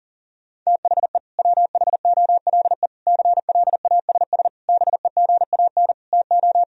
Morse code, heard as a keyed tone, about 40 wpm, 700 Hz. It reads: THE WHOLE CLASS BEGAN TO
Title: Sentences from Top 1000 Words - Review Part 2 - 40wpm